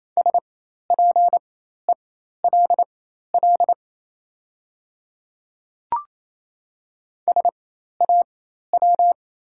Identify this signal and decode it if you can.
Morse code, heard as a keyed tone, about 28 wpm, spaced as for 13 wpm, 700 Hz. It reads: SPELL SAW
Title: 4x Character Spacing - Top 500 Words - 28wpm